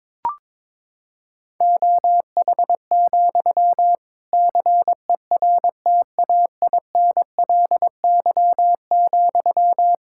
Morse code, heard as a keyed tone, about 22 wpm, 700 Hz. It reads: OH, CERTAINLY,
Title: Sentences from Top 1600 Words 22wpm